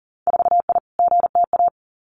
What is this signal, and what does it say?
Morse code, heard as a keyed tone, about 37 wpm, 700 Hz. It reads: 4I GTA
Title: Winter Field Day Exchanges Encore 40wpm